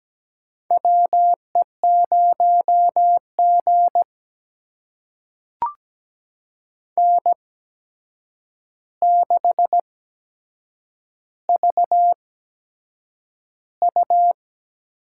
Morse code, heard as a keyed tone, about 17 wpm, spaced as for 7 wpm, 700 Hz. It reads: WE0G N6VU
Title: USA Call Signs - 8x Character Spacing - 17wpm